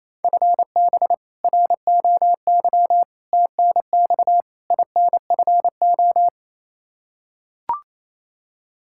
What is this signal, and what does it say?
Morse code, heard as a keyed tone, about 28 wpm, 700 Hz. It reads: FB ROY TNX INFO
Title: K1USN SST QSO Practice 28wpm